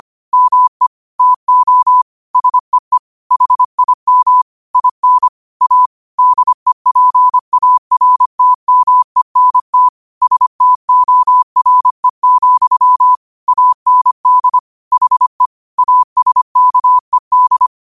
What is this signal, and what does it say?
Morse code, heard as a keyed tone, about 25 wpm, 1000 Hz. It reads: ME TO SEE HIM IN A DEPARTMENT STORE, AND HE ASKED
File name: qotdfiles / 25wpm